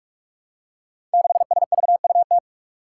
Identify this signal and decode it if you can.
Morse code, heard as a keyed tone, about 44 wpm, 700 Hz. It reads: 6SVUT
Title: Five Challenging Letter-Number Combinations 45wpm